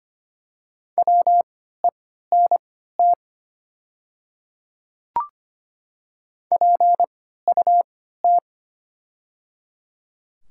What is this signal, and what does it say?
Morse code, heard as a keyed tone, about 25 wpm, spaced as for 14 wpm, 700 Hz. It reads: WENT PUT
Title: 3x Character Spacing - Top 200 Words - 25wpm